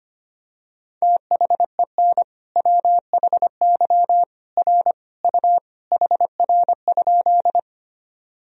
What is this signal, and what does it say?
Morse code, heard as a keyed tone, about 25 wpm, 700 Hz. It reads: THEN WHY R U HR?
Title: Sentences from Top 200 Words and Abbreviations 25wpm